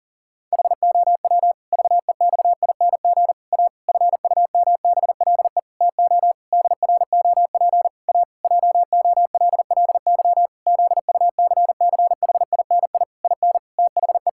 Morse code, heard as a keyed tone, about 38 wpm, 700 Hz. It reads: HOW VEXING A FUMBLE TO DROP A JOLLY ZUCCHINI IN THE
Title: Pangram Challenge v2 - 40wpm